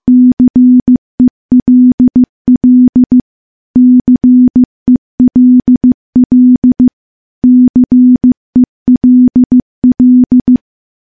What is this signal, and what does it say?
Morse code, heard as a keyed tone, about 15 wpm, 250 Hz. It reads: CELL CELL CELL
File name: morse / morse-training-15WPM